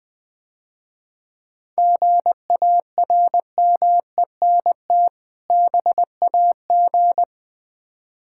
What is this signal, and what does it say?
Morse code, heard as a keyed tone, about 20 wpm, 700 Hz. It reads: GARMENT BAG
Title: Difficult Words - Speed-Racing with Hints - 20wpm